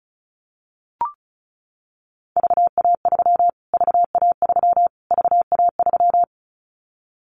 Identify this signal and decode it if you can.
Morse code, heard as a keyed tone, about 34 wpm, 700 Hz. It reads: VA3 VA3 VA3
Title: Callsign Prefixes for Top 25 DXCC - Repeated 3x - 35wpm